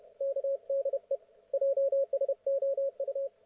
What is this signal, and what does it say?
Morse code, heard as a keyed tone, about 32 wpm, 550 Hz. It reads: KDE JSOU